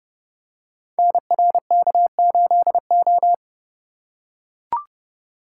Morse code, heard as a keyed tone, about 30 wpm, 700 Hz. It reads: NRK8O
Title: Code Groups - 5 Letter:Numbers - 30wpm